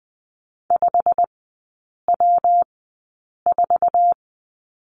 Morse code, heard as a keyed tone, about 20 wpm, spaced as for 8 wpm, 700 Hz. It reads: 5W4
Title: 114 Lesson 19e - Letter w - Three Characters 20wpm